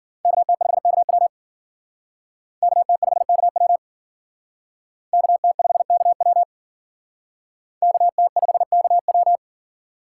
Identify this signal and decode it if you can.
Morse code, heard as a keyed tone, about 51 wpm, 700 Hz. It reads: KT5KW KT5KW KT5KW KT5KW